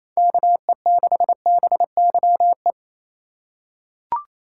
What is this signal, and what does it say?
Morse code, heard as a keyed tone, about 28 wpm, 700 Hz. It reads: KE6BYE
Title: USA Call Signs - 6 Characters - 2x Character Spacing - 28wpm